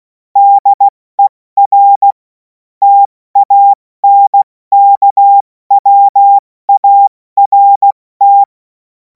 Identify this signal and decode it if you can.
Morse code, heard as a keyed tone, about 16 wpm, 800 Hz. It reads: DER TANKWART
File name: Morsekurs / Witze / Witz 05. Mai 2026 70 F80